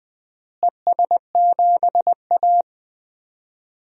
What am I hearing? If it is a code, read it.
Morse code, heard as a keyed tone, about 20 wpm, 700 Hz. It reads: ES7A
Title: DXCC-100 Call Signs - 4 Characters - Repeated 3x - 20wpm